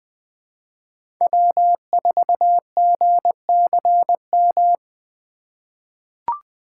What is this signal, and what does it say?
Morse code, heard as a keyed tone, about 20 wpm, 700 Hz. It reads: W4GCM
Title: USA Call Signs - 2x Character Spacing - 20wpm